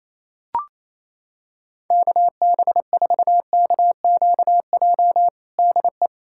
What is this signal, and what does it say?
Morse code, heard as a keyed tone, about 28 wpm, 700 Hz. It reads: KB4KQJ DE
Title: USA Callsign DE Callsign 28wpm